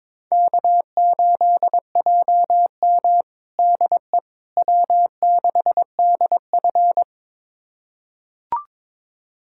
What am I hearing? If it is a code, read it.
Morse code, heard as a keyed tone, about 21 wpm, 700 Hz. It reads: K8JM DE W6DF
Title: USA Callsign DE Callsign - 4 Characters - 22wpm